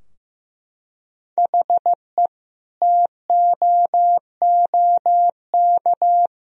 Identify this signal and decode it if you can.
Morse code, heard as a keyed tone, about 16 wpm, 700 Hz. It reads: HE TOOK
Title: Sentences from Top 1000 Words - Review Part 2 - 15wpm